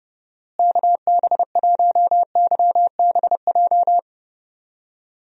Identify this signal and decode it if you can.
Morse code, heard as a keyed tone, about 30 wpm, 700 Hz. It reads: KB1YBJ
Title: USA Call Signs - 6 Characters - Repeated 2x - 30wpm